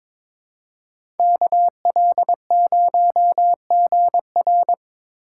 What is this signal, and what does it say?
Morse code, heard as a keyed tone, about 25 wpm, 700 Hz. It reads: KL0GR
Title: Speed Racing - USA 2x2 Call Signs 22wpm